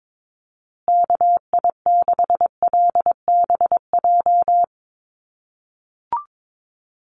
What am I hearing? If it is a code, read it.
Morse code, heard as a keyed tone, about 22 wpm, 700 Hz. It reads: KI6LBJ
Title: USA Call Signs Repeated 2x 22wpm